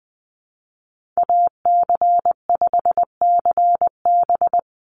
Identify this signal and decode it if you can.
Morse code, heard as a keyed tone, about 20 wpm, 700 Hz. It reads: AC5CB
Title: USA Call Signs 20wpm